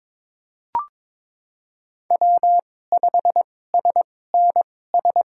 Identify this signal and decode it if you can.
Morse code, heard as a keyed tone, about 22 wpm, spaced as for 16 wpm, 700 Hz. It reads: W5SNS
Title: USA Call Signs - 5 Characters - 2x Character Spacing - 22wpm